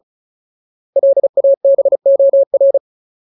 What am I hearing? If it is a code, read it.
Morse code, heard as a keyed tone, about 34 wpm, 550 Hz. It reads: LABOR
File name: Ultimate Word List / 35WPM Ultimate